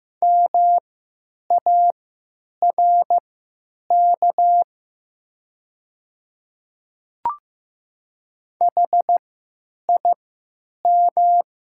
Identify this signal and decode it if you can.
Morse code, heard as a keyed tone, about 15 wpm, spaced as for 9 wpm, 700 Hz. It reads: MARK HIM
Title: 3x Character Spacing - Top 300 Words - 15wpm